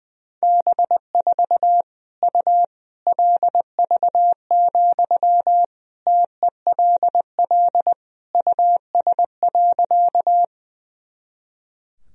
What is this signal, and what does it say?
Morse code, heard as a keyed tone, about 20 wpm, 700 Hz. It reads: B4 U LV, TELL US.
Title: Sentences from Top 400 Words and Abbreviations 20wpm